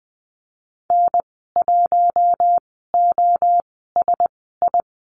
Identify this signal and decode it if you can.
Morse code, heard as a keyed tone, about 20 wpm, spaced as for 14 wpm, 700 Hz. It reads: N1OSI
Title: 042 Lesson 8g - Number 1 - Call Signs 20wpm